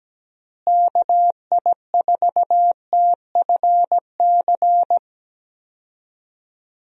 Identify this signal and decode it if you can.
Morse code, heard as a keyed tone, about 17 wpm, 700 Hz. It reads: KI4TFC
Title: USA Call Signs - 6 Characters - Repeated 3x - 17wpm